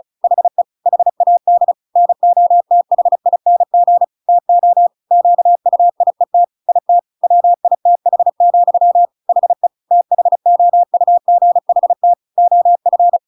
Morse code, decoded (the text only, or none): HE HAD NOTHING TO QUIET IT WITH, HE THOUGHT OF